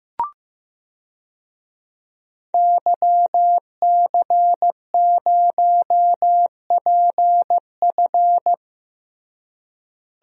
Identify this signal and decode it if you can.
Morse code, heard as a keyed tone, about 15 wpm, 700 Hz. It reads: YC0PF
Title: DXCC-100 Call Signs - 5 Characters - Repeated 2x - 15wpm